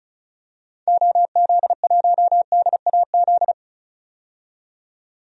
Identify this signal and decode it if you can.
Morse code, heard as a keyed tone, about 36 wpm, 700 Hz. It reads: OZ1DAZ
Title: DXCC-100 Call Signs - 6 Characters - 35wpm